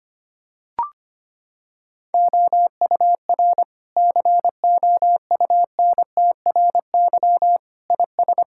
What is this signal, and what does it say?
Morse code, heard as a keyed tone, about 25 wpm, 700 Hz. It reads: OUR COUNTRY IS
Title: Sentences from Top 1000 Words - Review Part 6 - 25wpm